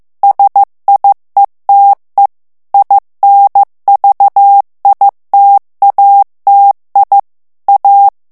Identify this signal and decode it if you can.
Morse code, heard as a keyed tone, about 15 wpm, 800 Hz. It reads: SIETE INVITATI A
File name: ITALIEN texte clair 1 vitesse 14 mots minute